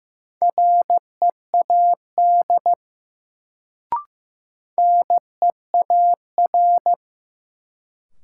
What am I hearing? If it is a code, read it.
Morse code, heard as a keyed tone, about 15 wpm, 700 Hz. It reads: READ NEAR
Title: Sentences from Top 200 Words - New Word Drill - 15wpm